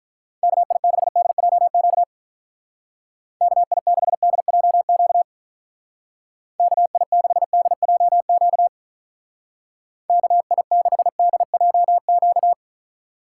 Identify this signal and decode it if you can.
Morse code, heard as a keyed tone, about 42 wpm, 700 Hz. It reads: KI6DJQ KI6DJQ KI6DJQ KI6DJQ
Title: Speed Racing - USA 2x3 Call Signs 35wpm